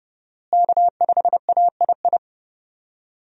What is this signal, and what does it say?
Morse code, heard as a keyed tone, about 30 wpm, 700 Hz. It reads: K5AII
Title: USA Call Signs - 5 Characters - Repeated 2x - 30wpm